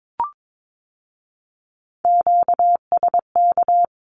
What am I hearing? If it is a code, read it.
Morse code, heard as a keyed tone, about 22 wpm, 700 Hz. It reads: QSK
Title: Q-Codes with Spoken Meaning - 22wpm